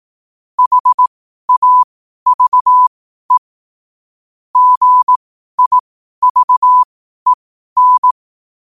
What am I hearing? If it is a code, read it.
Morse code, heard as a keyed tone, about 18 wpm, spaced as for 12 wpm, 1000 Hz. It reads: HAVE GIVEN
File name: qotdfiles / 12wpm